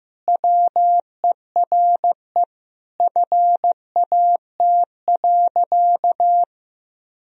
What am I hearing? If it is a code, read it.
Morse code, heard as a keyed tone, about 15 wpm, 700 Hz. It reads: WERE FAT.
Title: Sentences from Top 1000 Words - Review Part 4 - 15wpm